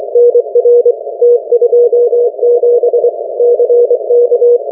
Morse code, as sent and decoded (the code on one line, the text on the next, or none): -. .-. / - ..--- --... / -.-. -.-
NR T27 CK